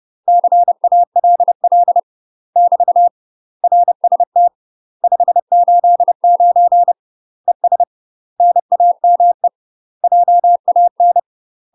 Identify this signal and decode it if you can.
Morse code, heard as a keyed tone, about 30 wpm, 700 Hz. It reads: CALL <BT> RST 589 ES NAME JAN
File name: UK MASTER QSOs TRAINING / 30WPM / UK MASTER QSOs 0026